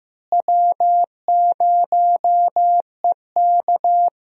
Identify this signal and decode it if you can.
Morse code, heard as a keyed tone, about 15 wpm, 700 Hz. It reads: W0EK
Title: USA Call Signs - 4 Characters - Repeated 3x - 15wpm